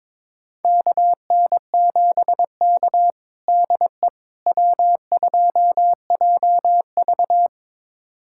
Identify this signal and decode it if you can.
Morse code, heard as a keyed tone, about 22 wpm, 700 Hz. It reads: KN7K DE W2JV
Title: USA Callsign DE Callsign - 4 Characters - 22wpm